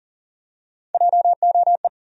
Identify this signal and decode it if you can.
Morse code, heard as a keyed tone, about 40 wpm, 700 Hz. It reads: JOE
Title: USA Common Names - Max 5 Letters - 40wpm